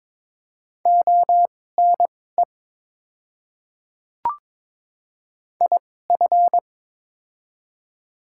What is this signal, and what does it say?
Morse code, heard as a keyed tone, about 22 wpm, spaced as for 16 wpm, 690 Hz. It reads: ONE IF
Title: 2x Character Spacing - Top 100 Words - 22wpm